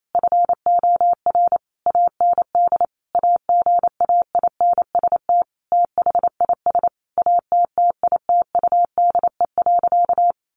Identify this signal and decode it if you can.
Morse code, heard as a keyed tone, about 28 wpm, 700 Hz. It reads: FOR AND AGAINST THIS ATTITUDE.
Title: Sentences from Top 1300 Words 28wpm